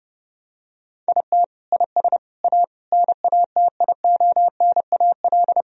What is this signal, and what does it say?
Morse code, heard as a keyed tone, about 31 wpm, 700 Hz. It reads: IT IS A NATIONAL
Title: Sentences from Top 1000 Words - Review Part 4 - 30wpm